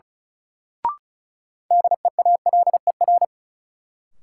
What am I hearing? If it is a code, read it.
Morse code, heard as a keyed tone, about 36 wpm, 700 Hz. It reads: DEALER
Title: Sentences from Top 1100 Words - New Word Drill Alt - 35wpm